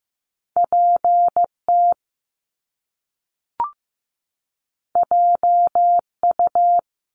Morse code, heard as a keyed tone, about 15 wpm, 700 Hz. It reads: PT JU
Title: Top Two Letter Combinations in Words 15wpm